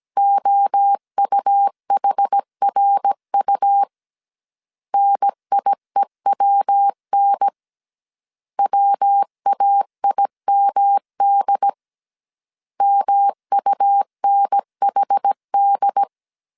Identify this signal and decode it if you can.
Morse code, heard as a keyed tone, about 17 wpm, 800 Hz. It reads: OUHRU NIEWN WAIMD MUNHD